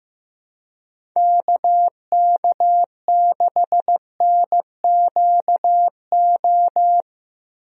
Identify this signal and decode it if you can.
Morse code, heard as a keyed tone, about 15 wpm, 700 Hz. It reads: KK6NQO
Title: USA Call Signs - 6 Characters - Repeated 2x - 15wpm